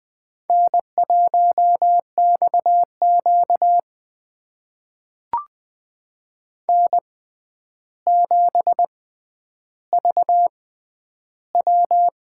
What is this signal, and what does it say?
Morse code, heard as a keyed tone, about 20 wpm, spaced as for 10 wpm, 700 Hz. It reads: N1XQ N7VW
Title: USA Call Signs - 6x Character Spacing - 20wpm